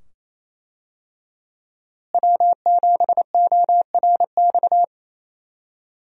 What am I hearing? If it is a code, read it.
Morse code, heard as a keyed tone, about 28 wpm, 700 Hz. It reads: W7ORX
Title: USA Call Signs - 5 Characters - Repeated 2x - 28wpm